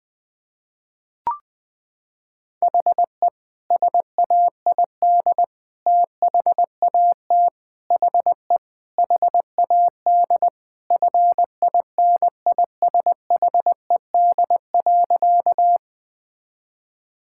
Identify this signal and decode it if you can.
Morse code, heard as a keyed tone, about 21 wpm, 700 Hz. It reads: HE SAID THAT HE HAD FINISHED.
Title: Sentences with 3-Word Phrases - 20wpm